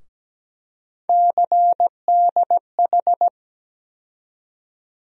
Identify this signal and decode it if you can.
Morse code, heard as a keyed tone, about 17 wpm, 700 Hz. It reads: CDH
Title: USA Call Sign Postfixes - 3 Letters 17wpm